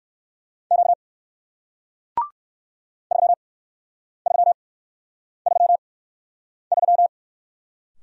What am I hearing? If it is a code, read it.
Morse code, heard as a keyed tone, about 58 wpm, 700 Hz. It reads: Q 3 3 3 3